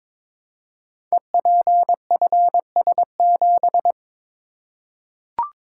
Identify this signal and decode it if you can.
Morse code, heard as a keyed tone, about 23 wpm, 690 Hz. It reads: EPFS7
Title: Code Groups - 5 Letter:Numbers - 22wpm